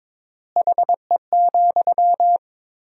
Morse code, heard as a keyed tone, about 23 wpm, 700 Hz. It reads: HE,
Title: 4x Word-Spacing - Sentences from Top 500 Words - 22wpm